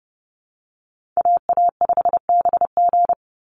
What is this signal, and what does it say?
Morse code, heard as a keyed tone, about 30 wpm, 700 Hz. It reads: AA5BG
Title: Call Signs - 30wpm